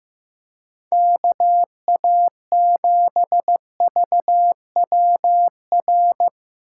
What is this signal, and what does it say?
Morse code, heard as a keyed tone, about 15 wpm, 700 Hz. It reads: KA7VWR
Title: USA Call Signs - 6 Characters - 3x Character Spacing - 15wpm